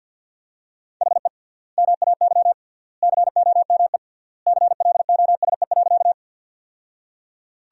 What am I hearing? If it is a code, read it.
Morse code, heard as a keyed tone, about 50 wpm, 700 Hz. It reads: HE MAY COME CLOSE.
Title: 3x Word-Spacing - Sentences from Top 300 Words - 50wpm